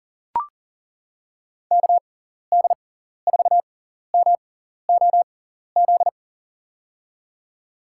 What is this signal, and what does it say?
Morse code, heard as a keyed tone, about 40 wpm, spaced as for 14 wpm, 700 Hz. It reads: KD4MOZ
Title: USA Call Signs - 6 Characters - 6x Character Spacing - 40wpm